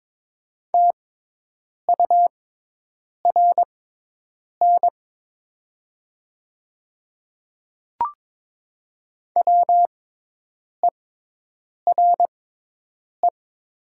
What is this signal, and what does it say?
Morse code, heard as a keyed tone, about 22 wpm, spaced as for 8 wpm, 700 Hz. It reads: TURN WERE
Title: 6x Character Spacing - Top 200 Words - 22wpm